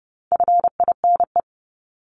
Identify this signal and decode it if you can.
Morse code, heard as a keyed tone, about 30 wpm, 700 Hz. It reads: FINE
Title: Top 500 Words Repeated 30wpm